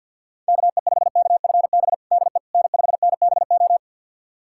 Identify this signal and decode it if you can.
Morse code, heard as a keyed tone, about 52 wpm, 700 Hz. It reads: KE5KFB DE N5NBO
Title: USA Callsign DE Callsign 50wpm